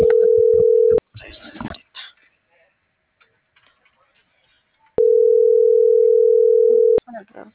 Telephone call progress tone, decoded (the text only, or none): ringback tone